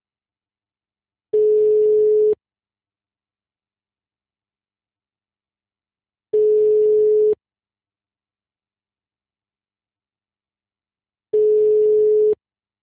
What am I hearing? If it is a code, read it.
Ringback tone (Europe, 425 Hz).